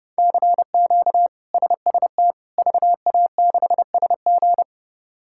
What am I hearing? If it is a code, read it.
Morse code, heard as a keyed tone, about 30 wpm, 700 Hz. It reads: CQ SST VA6SG
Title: K1USN SST QSO Practice - 30wpm